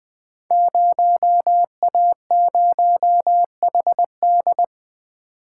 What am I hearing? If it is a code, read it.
Morse code, heard as a keyed tone, about 20 wpm, 700 Hz. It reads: 0A0HD
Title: Code Groups - 5 Letter:Numbers - 20wpm